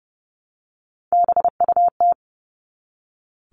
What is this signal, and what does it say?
Morse code, heard as a keyed tone, about 30 wpm, 700 Hz. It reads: BUT
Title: Top 300 Words 30wpm